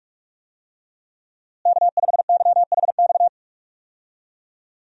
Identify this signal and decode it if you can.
Morse code, heard as a keyed tone, about 46 wpm, 700 Hz. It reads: K5YHX
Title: USA Call Signs - 5 Characters - Repeated 3x - 45wpm